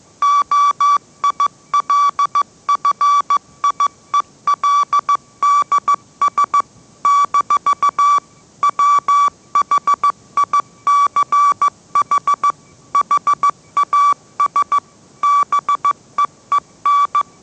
Morse code, heard as a keyed tone, about 15 wpm, 1150 Hz. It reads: OILFIELDS - WHICH HAS BEEN
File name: mrx / mrx065 16wpm